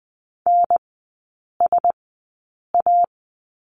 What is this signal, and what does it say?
Morse code, heard as a keyed tone, about 20 wpm, spaced as for 8 wpm, 700 Hz. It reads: NSA
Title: 074 Lesson 13e - Letter L - Three Characters 20wpm